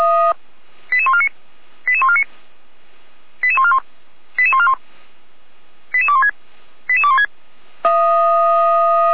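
Five-tone selective call (ZVEI), heard as quick runs of five tones, tones of about 70 ms. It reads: 88149 88149 88141 88141 88216 88216
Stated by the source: ZVEI five-tone call 88217 received on an SDR.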